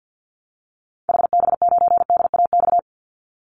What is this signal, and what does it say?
Morse code, heard as a keyed tone, about 49 wpm, 700 Hz. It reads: HB9DAX
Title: Call Signs Repeated 2x - 50wpm